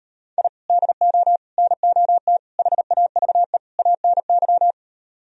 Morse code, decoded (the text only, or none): I DO NOT HAVE ANY